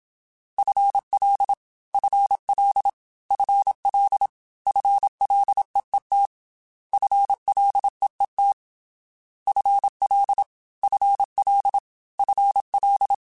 Morse code, heard as a keyed tone, about 27 wpm, 800 Hz. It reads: FL FL FL FLEET FLEET FL FL FL
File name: FL / FL mp3s / FL Prefixes 3x2 Twice @25